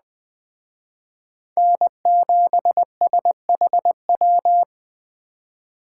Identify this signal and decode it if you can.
Morse code, heard as a keyed tone, about 20 wpm, 700 Hz. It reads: N7SHW
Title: Speed Racing - USA 1x3 Call Signs 17wpm